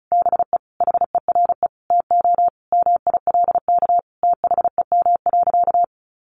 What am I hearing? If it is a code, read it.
Morse code, heard as a keyed tone, about 34 wpm, 700 Hz. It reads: BE HERE TO MILK THEM.
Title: Sentences from Top 1500 Words 35wpm